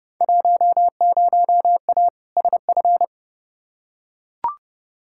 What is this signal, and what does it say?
Morse code, heard as a keyed tone, about 31 wpm, 700 Hz. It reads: 10A SF
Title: Field Day Exchanges Encore - 30wpm